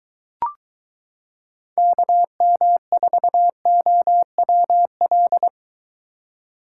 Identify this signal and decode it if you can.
Morse code, heard as a keyed tone, about 23 wpm, 700 Hz. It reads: KM4OWL